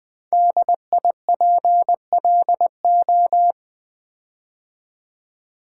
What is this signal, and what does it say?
Morse code, heard as a keyed tone, about 20 wpm, 700 Hz. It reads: DIPLO
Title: Build A Word - 5 Letters At A Time - 10-Letter Words - 20wpm